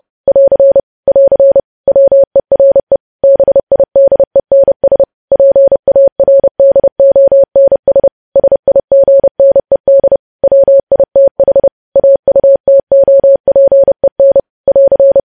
Morse code, heard as a keyed tone, about 30 wpm, 550 Hz. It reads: <AR> <AR> WERE BIDENS PARDONS SIGNED WITH AUTOPEN <AR>
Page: news headlines in morse code at 30 wpm